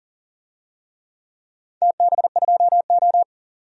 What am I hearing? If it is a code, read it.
Morse code, heard as a keyed tone, about 39 wpm, 700 Hz. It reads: TB2O
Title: DXCC-100 Call Signs - 4 Characters - 40wpm